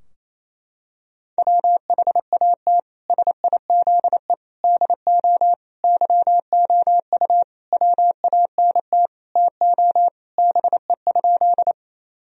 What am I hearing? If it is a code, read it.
Morse code, heard as a keyed tone, about 28 wpm, 700 Hz. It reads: WHAT SIZE DO YOU WANT TO BE?
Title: Sentences from Top 1000 Words - Review Part 2 - 28wpm